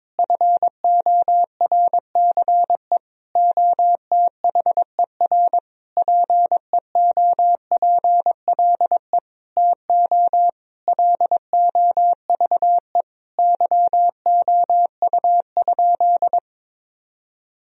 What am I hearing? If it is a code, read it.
Morse code, heard as a keyed tone, about 22 wpm, 700 Hz. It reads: FORCE OTHER PEOPLE TO LOVE YOU?
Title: Sentences from Top 1000 Words - Review Part 3 - 22wpm